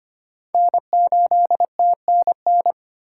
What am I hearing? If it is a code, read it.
Morse code, heard as a keyed tone, about 25 wpm, 700 Hz. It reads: N8TNN
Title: USA Call Signs - 5 Characters - 3x Character Spacing - 25wpm